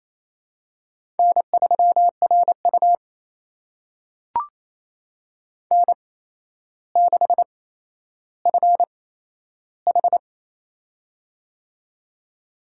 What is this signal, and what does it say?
Morse code, heard as a keyed tone, about 28 wpm, spaced as for 12 wpm, 700 Hz. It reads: N3RU N6FH
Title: USA Call Signs - 4 Characters - 8x Character Spacing - 28wpm